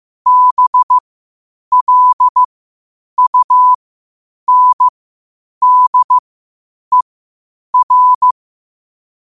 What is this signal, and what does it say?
Morse code, heard as a keyed tone, about 15 wpm, spaced as for 8 wpm, 1000 Hz. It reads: BLUNDER
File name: qotdfiles / 8wpm